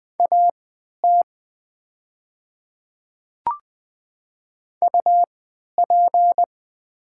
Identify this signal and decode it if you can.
Morse code, heard as a keyed tone, about 20 wpm, spaced as for 11 wpm, 700 Hz. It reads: AT UP
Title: 3x Character Spacing - Top 100 Words - 20wpm